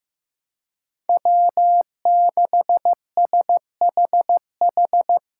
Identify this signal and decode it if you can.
Morse code, heard as a keyed tone, about 16 wpm, 700 Hz. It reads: W6SHH
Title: USA Callsign DE Callsign - 5 Characters - 15wpm